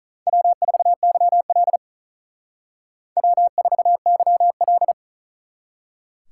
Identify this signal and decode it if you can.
Morse code, heard as a keyed tone, about 46 wpm, 700 Hz. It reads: W4YL W4YL